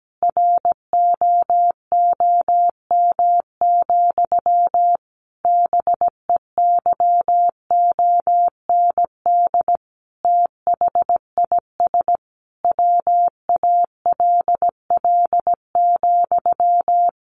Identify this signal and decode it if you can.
Morse code, heard as a keyed tone, about 17 wpm, 700 Hz. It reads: ROOM, BEYOND THIS WALL,
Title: A Sound Of Thunder 17wpm